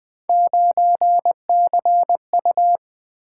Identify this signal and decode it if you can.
Morse code, heard as a keyed tone, about 20 wpm, 700 Hz. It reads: 9CU
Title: Three Letter-Number Combinations - 20wpm